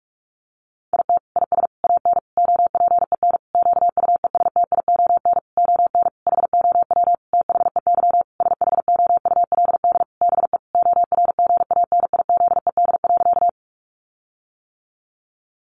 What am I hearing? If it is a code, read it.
Morse code, heard as a keyed tone, about 44 wpm, 700 Hz. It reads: IT IS AN OPEN QUESTION ON HOW THEY SHOULD BE ORGANIZED.
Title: Sentences from Top 1300 Words 45wpm